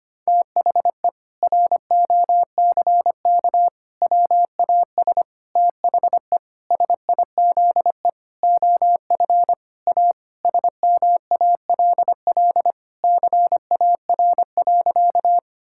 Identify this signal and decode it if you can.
Morse code, heard as a keyed tone, about 25 wpm, 700 Hz. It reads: THE ROCK WAS THE SIZE OF A SMALL CAR.
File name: practice / 25wpm / Sentences with 4-Word Phrases - 25wpm